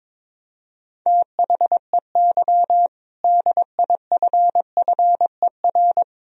Morse code, heard as a keyed tone, about 22 wpm, 700 Hz. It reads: THEY DIFFER
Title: Sentences from Top 1000 Words - Review Part 2 - 22wpm